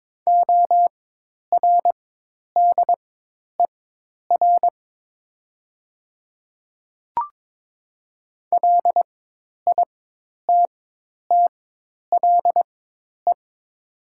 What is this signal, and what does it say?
Morse code, heard as a keyed tone, about 22 wpm, spaced as for 10 wpm, 700 Hz. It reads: ORDER LITTLE
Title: 4x Character Spacing - Top 500 Words - 22wpm